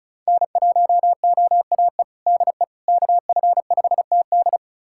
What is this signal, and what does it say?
Morse code, heard as a keyed tone, about 34 wpm, 700 Hz. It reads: N1OAE DE KF5TD
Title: USA Callsign DE Callsign - 5 Characters - 35wpm